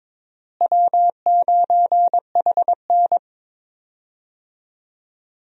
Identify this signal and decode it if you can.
Morse code, heard as a keyed tone, about 21 wpm, 700 Hz. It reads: W9HN
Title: USA Call Signs - 4 Characters - Repeated 3x - 22wpm